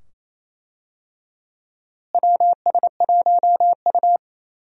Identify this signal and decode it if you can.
Morse code, heard as a keyed tone, about 28 wpm, 700 Hz. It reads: WS1U
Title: USA Call Signs - 4 Characters - Repeated 2x - 28wpm